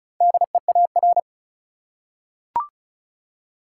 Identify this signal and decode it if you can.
Morse code, heard as a keyed tone, about 36 wpm, 700 Hz. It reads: DEAR
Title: Four Letter Words - 35wpm